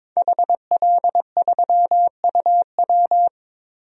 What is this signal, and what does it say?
Morse code, heard as a keyed tone, about 22 wpm, 700 Hz. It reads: HL3UW
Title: DXCC-100 Call Signs - 5 Characters - Repeated 2x - 22wpm